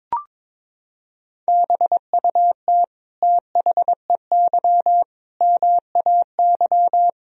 Morse code, heard as a keyed tone, about 22 wpm, 700 Hz. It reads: BUT THEY MAY
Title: Sentences from Top 300 Words and Abbreviations 22wpm